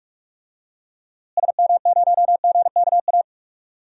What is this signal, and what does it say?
Morse code, heard as a keyed tone, about 41 wpm, 700 Hz. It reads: SM0GKA
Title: DXCC-100 Call Signs - 6 Characters - Repeated 3x - 45wpm